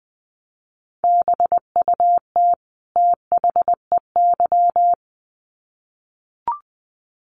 Morse code, heard as a keyed tone, about 20 wpm, 700 Hz. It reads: BUT THEY
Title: Sets of 2 Words 20wpm